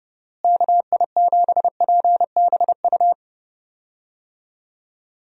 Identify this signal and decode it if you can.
Morse code, heard as a keyed tone, about 30 wpm, 700 Hz. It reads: KI7PBU
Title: USA Call Signs - 6 Characters - Repeated 3x - 30wpm